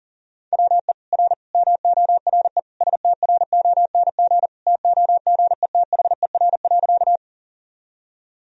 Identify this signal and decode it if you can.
Morse code, heard as a keyed tone, about 40 wpm, 700 Hz. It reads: WE R MORE STRONG TOGETHER.
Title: Sentences from Top 500 Words and Abbreviations 40wpm